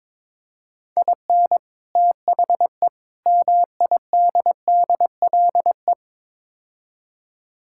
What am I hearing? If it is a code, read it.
Morse code, heard as a keyed tone, about 22 wpm, 700 Hz. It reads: IN THE MIDDLE
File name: practice / 22wpm / Build A Phrase - 5-Word Phrases - 22wpm